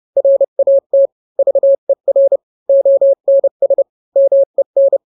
Morse code, heard as a keyed tone, about 30 wpm, 550 Hz. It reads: RAT VER ONS MEN